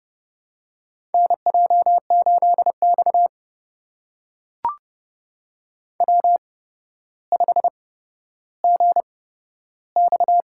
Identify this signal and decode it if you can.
Morse code, heard as a keyed tone, about 30 wpm, spaced as for 13 wpm, 700 Hz. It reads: NJ8X W5GX
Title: USA Call Signs - 4 Characters - 8x Character Spacing - 30wpm